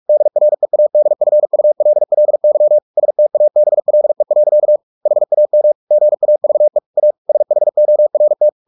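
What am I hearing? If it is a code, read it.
Morse code, heard as a keyed tone, about 44 wpm, 600 Hz. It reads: DREADFULLY STABLE. HAM GAVE A SHORT
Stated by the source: Morse practice at 45 wpm, the novel Never by Ken Follett in English, from CW med Gnister.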